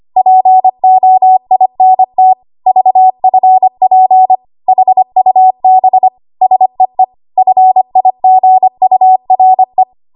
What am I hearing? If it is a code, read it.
Morse code, heard as a keyed tone, about 25 wpm, 750 Hz. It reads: POINT VFP HUB SEE FIGURE
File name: Archive / 25 WPM / 25WPM